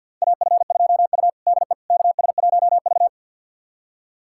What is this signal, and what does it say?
Morse code, heard as a keyed tone, about 52 wpm, 700 Hz. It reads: AF2U DE KS1V
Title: USA Callsign DE Callsign - 4 Characters - 50wpm